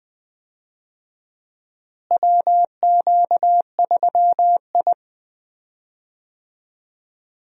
Morse code, heard as a keyed tone, about 20 wpm, 700 Hz. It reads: WQ3I
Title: USA Call Signs - 4 Characters - Repeated 2x - 20wpm